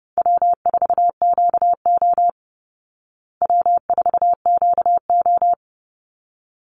Morse code, heard as a keyed tone, about 30 wpm, 700 Hz. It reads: W4QO W4QO
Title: Easy Call Signs Repeated 2x - 30wpm